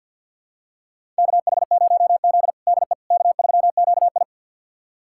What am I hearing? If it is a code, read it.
Morse code, heard as a keyed tone, about 46 wpm, 700 Hz. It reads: KH0Z DE K3QI
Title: USA Callsign DE Callsign - 4 Characters - 50wpm